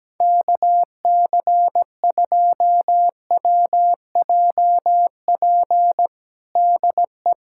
Morse code, heard as a keyed tone, about 17 wpm, 700 Hz. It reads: KC2WJP DE
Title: USA Callsign DE Callsign - 6 Characters - 17wpm